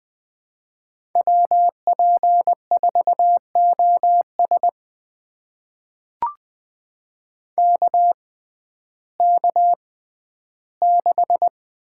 Morse code, heard as a keyed tone, about 20 wpm, spaced as for 12 wpm, 700 Hz. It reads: WP4OS KK6
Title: USA Call Signs - 5 Characters - 6x Character Spacing - 20wpm